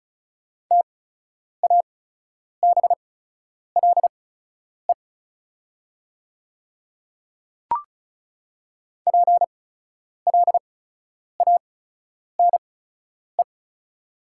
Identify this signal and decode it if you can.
Morse code, heard as a keyed tone, about 34 wpm, spaced as for 10 wpm, 700 Hz. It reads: TABLE PLANE
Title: 8x Character Spacing - Top 500 Words - 35wpm